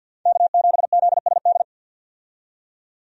Morse code, heard as a keyed tone, about 52 wpm, 700 Hz. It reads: K7ZSD
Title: USA Call Signs - 5 Characters - 50wpm